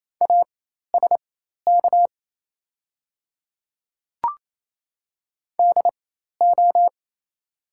Morse code, heard as a keyed tone, about 28 wpm, spaced as for 13 wpm, 690 Hz. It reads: ASK DO
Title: 4x Character Spacing - Top 200 Words - 28wpm